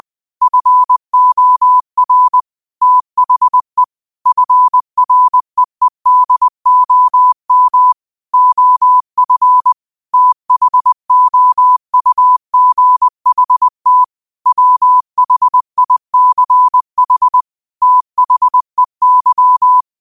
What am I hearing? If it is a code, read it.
Morse code, heard as a keyed tone, about 20 wpm, 1000 Hz. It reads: FOR THE FREEDOM OF THOUGHT WHICH THEY